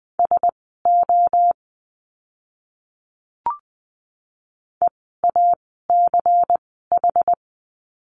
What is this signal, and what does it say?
Morse code, heard as a keyed tone, about 20 wpm, spaced as for 14 wpm, 700 Hz. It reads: SO EACH
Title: 230 Lesson 37f - Letter x - Words 20wpm